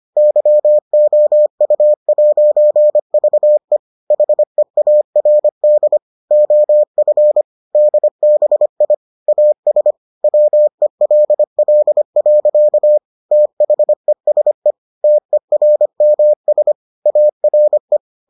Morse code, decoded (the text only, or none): YOU'VE HEARD OF DBI AS WELL. THESE TERMS ARE